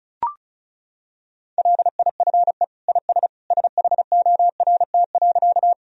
Morse code, heard as a keyed tone, about 36 wpm, 700 Hz. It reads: LIFE IS SHORT.
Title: Sentences from Top 1000 Words - Review Part 2 - 35wpm